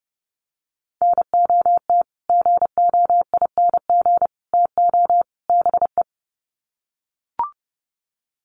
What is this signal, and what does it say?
Morse code, heard as a keyed tone, about 29 wpm, 700 Hz. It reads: NOT GOING TO BE
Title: Sets of 4 Words 30wpm